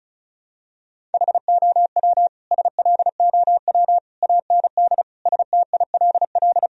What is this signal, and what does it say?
Morse code, heard as a keyed tone, about 36 wpm, 700 Hz. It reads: HOW SLOW AND STILL
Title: Sentences from Top 1000 Words - Review Part 2 - 35wpm